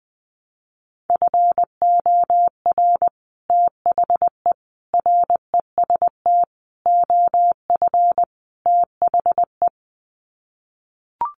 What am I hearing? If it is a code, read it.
Morse code, heard as a keyed tone, about 20 wpm, 700 Hz. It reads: FOR THE REST OF THE
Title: Sets of 5 Words 20wpm